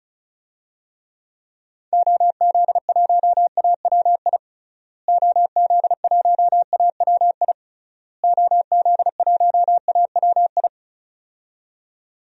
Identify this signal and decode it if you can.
Morse code, heard as a keyed tone, about 36 wpm, 700 Hz. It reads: OZ1AWI OZ1AWI OZ1AWI
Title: DXCC-100 Call Signs - 6 Characters - Repeated 3x - 35wpm